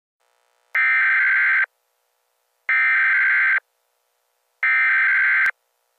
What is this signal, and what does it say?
Emergency Alert System (SAME) tones (header bursts), heard as harsh buzzing bursts. It is heard from 0.8 to 5.5 s.